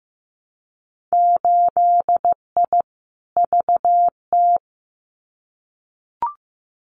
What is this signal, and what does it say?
Morse code, heard as a keyed tone, about 15 wpm, 700 Hz. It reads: 8I VT
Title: Winter Field Day Exchanges Encore 15wpm